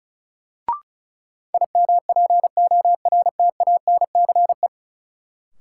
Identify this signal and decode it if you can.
Morse code, heard as a keyed tone, about 36 wpm, 700 Hz. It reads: IMPORTANCE
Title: Sets of 3 Words - New Word Drill - 35wpm